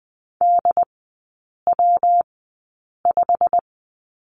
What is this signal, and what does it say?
Morse code, heard as a keyed tone, about 20 wpm, spaced as for 8 wpm, 700 Hz. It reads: DW5
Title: 127 Lesson 21e - Number 6 - Three Characters 20wpm